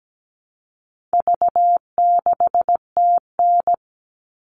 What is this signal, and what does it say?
Morse code, heard as a keyed tone, about 17 wpm, 700 Hz. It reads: V6TN
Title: All Cut Numbers - 4-Digit 17wpm